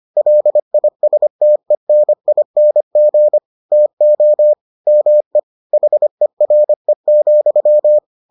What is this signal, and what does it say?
Morse code, heard as a keyed tone, about 25 wpm, 600 Hz. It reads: LISTENING TO ME HERE,